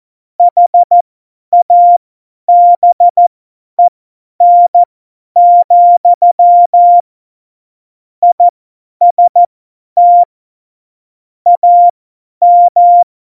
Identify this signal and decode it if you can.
Morse code, heard as a keyed tone, about 14 wpm, spaced as for 10 wpm, 700 Hz. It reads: HABEN, IST AM
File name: Morsekurs / Witze / Witz 01. September 2020, 50 F70